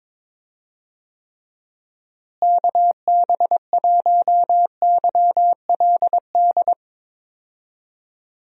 Morse code, heard as a keyed tone, about 22 wpm, 700 Hz. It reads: KB1YLD
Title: USA Call Signs - 6 Characters - Repeated 2x - 22wpm